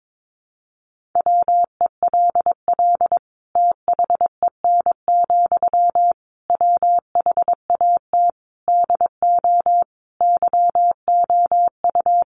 Morse code, heard as a keyed tone, about 22 wpm, 700 Hz. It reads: WELL THEN, WHAT DO YOU
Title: Sentences from Top 200 Words 22wpm